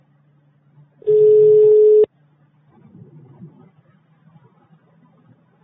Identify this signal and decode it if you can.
Ringback tone (Europe, 425 Hz).